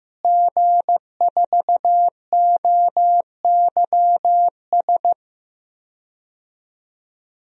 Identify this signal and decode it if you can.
Morse code, heard as a keyed tone, about 15 wpm, 700 Hz. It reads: G4OYS
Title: DXCC-100 Call Signs - 5 Characters - Repeated 2x - 15wpm